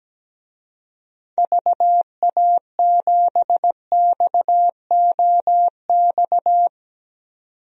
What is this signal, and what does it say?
Morse code, heard as a keyed tone, about 17 wpm, 700 Hz. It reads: VA7XOX
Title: DXCC-100 Call Signs - 6 Characters - 17wpm